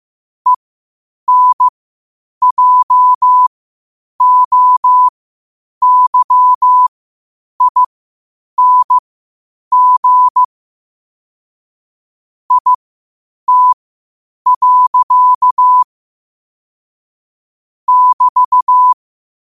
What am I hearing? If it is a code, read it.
Morse code, heard as a keyed tone, about 15 wpm, spaced as for 8 wpm, 1000 Hz. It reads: ENJOYING IT. <BT>